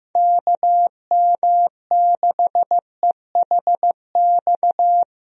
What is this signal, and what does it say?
Morse code, heard as a keyed tone, about 15 wpm, 700 Hz. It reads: KM6EHX
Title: USA Call Signs - 6 Characters - Repeated 2x - 15wpm